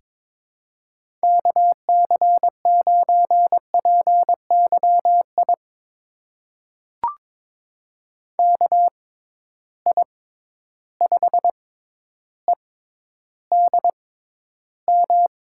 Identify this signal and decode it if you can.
Morse code, heard as a keyed tone, about 21 wpm, spaced as for 11 wpm, 700 Hz. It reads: KC9PYI KI5EDM
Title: USA Call Signs - 6 Characters - 6x Character Spacing - 22wpm